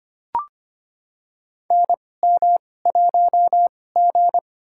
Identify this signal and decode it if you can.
Morse code, heard as a keyed tone, about 25 wpm, spaced as for 18 wpm, 700 Hz. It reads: NM1G
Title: USA Call Signs - 4 Characters - 2x Character Spacing - 25wpm